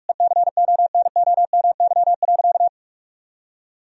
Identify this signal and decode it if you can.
Morse code, heard as a keyed tone, about 46 wpm, 700 Hz. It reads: ECONOMY.
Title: Sentences from Top 1000 Words - Review Part 3 - 45wpm